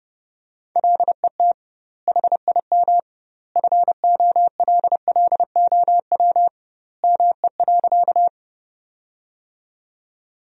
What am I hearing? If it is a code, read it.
Morse code, heard as a keyed tone, about 30 wpm, 700 Hz. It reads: LET HIM FOLLOW ME.
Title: 2x Word-Spacing - Sentences from Top 300 Words - 30wpm